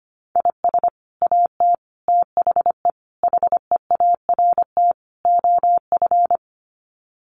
Morse code, heard as a keyed tone, about 25 wpm, 700 Hz. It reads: IS AT THE HEART OF
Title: Sets of 5 Words Encore2 25wpm